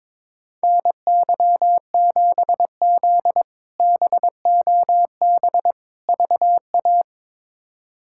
Morse code, heard as a keyed tone, about 21 wpm, 700 Hz. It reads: NY7Z BOB VA
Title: K1USN SST QSO Practice - 22wpm